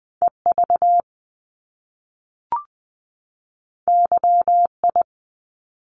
Morse code, heard as a keyed tone, about 20 wpm, 700 Hz. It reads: EV YI
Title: Top Two Letter Combinations in Words 20wpm